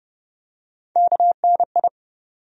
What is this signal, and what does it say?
Morse code, heard as a keyed tone, about 30 wpm, 700 Hz. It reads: KNI
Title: USA Call Sign Postfixes - 3 Letters 30wpm